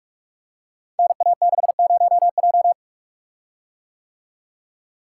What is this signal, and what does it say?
Morse code, heard as a keyed tone, about 44 wpm, 700 Hz. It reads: NA60J
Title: Code Groups - 5 Letter:Numbers - 45wpm